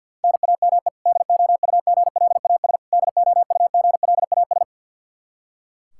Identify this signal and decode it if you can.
Morse code, heard as a keyed tone, about 50 wpm, 700 Hz. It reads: NAME DOUGLAS DOUGLAS
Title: Practice QSO - Exchange 1 - Beginning - 50wpm